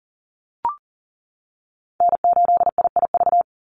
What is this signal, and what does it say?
Morse code, heard as a keyed tone, about 40 wpm, 700 Hz. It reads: N8IIV
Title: USA Call Signs 40wpm